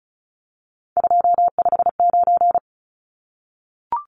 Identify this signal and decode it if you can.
Morse code, heard as a keyed tone, about 34 wpm, 700 Hz. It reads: 259
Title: Numbers - 3-Digit 35wpm